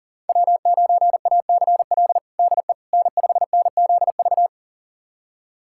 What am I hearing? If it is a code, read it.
Morse code, heard as a keyed tone, about 40 wpm, 700 Hz. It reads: W9ACL DE N5NZV
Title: USA Callsign DE Callsign - 5 Characters - 40wpm